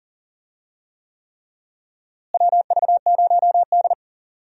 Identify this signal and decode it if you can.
Morse code, heard as a keyed tone, about 37 wpm, 700 Hz. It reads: WV0D
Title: USA Call Signs - 4 Characters - Repeated 3x - 40wpm